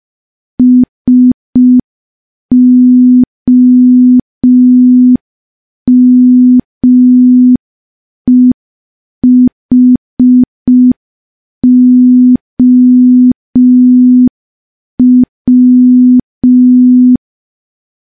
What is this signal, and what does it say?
Morse code, heard as a keyed tone, about 5 wpm, 250 Hz. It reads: SOMEHOW
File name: morse / morse-training-05WPM